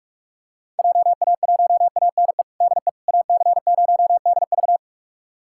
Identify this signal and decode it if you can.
Morse code, heard as a keyed tone, about 46 wpm, 700 Hz. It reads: JA1ANE DE AC0DV
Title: USA Callsign DE Callsign 45wpm